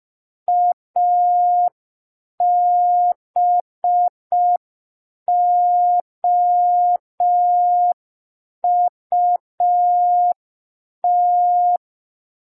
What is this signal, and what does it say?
Morse code, heard as a keyed tone, about 5 wpm, 710 Hz. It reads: ABOUT
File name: g3zpd / Morse code P2